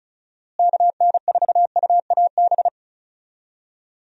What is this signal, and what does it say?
Morse code, heard as a keyed tone, about 36 wpm, 700 Hz. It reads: KN4UAB
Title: USA Call Signs - 6 Characters - Repeated 2x - 35wpm